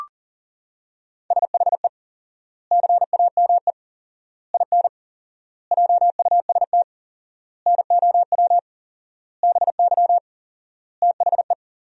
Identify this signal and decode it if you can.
Morse code, heard as a keyed tone, about 40 wpm, 700 Hz. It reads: SHE CAME IN JUST NOW BY THE
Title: 4x Word-Spacing - Sentences from Top 300 Words - 40wpm